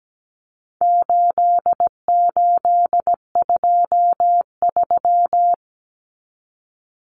Morse code, heard as a keyed tone, about 17 wpm, 700 Hz. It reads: 8823
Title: Numbers - 4-Digit 17wpm